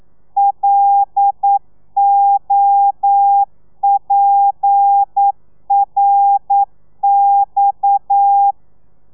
Morse code, heard as a keyed tone, about 9 wpm, 800 Hz. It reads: LOPRX
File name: telegrafia / i2tfj / Lez7 40